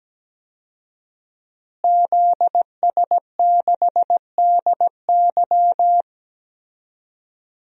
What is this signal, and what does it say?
Morse code, heard as a keyed tone, about 17 wpm, 700 Hz. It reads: ZS6DY
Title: DXCC-100 Call Signs - 5 Characters - Repeated 2x - 17wpm